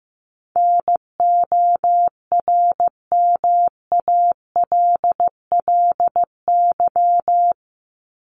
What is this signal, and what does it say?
Morse code, heard as a keyed tone, about 14 wpm, 700 Hz. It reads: NORMALLY
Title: LY Suffix - Top 100 Words with Base Words 15wpm